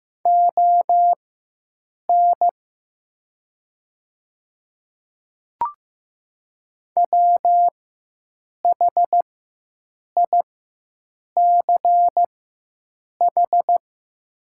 Morse code, heard as a keyed tone, about 15 wpm, spaced as for 7 wpm, 700 Hz. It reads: ON WHICH
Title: 4x Character Spacing - Top 100 Words - 15wpm